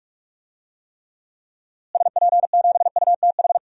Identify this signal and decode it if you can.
Morse code, heard as a keyed tone, about 46 wpm, 700 Hz. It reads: SP7UTH
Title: DXCC-100 Call Signs - 6 Characters - Repeated 2x - 45wpm